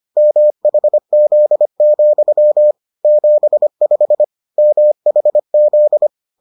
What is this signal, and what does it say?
Morse code, heard as a keyed tone, about 25 wpm, 600 Hz. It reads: MHZ, 75 MHZ